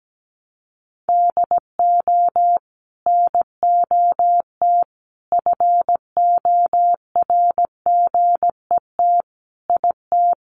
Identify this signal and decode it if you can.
Morse code, heard as a keyed tone, about 17 wpm, 700 Hz. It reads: DO NOT FORGET IT
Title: Sentences from Top 1200 Words 17wpm